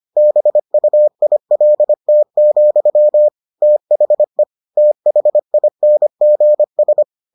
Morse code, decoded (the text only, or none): BUILT, THE THINGS